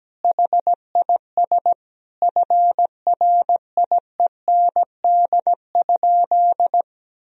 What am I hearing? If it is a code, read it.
Morse code, heard as a keyed tone, about 17 wpm, 700 Hz. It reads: HIS FRIEND?
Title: Sentences from Top 1000 Words - Review Part 3 - 17wpm